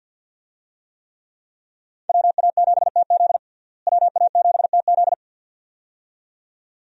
Morse code, heard as a keyed tone, about 46 wpm, 700 Hz. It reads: WA7TZ WA7TZ
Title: USA Call Signs - 5 Characters - Repeated 2x - 50wpm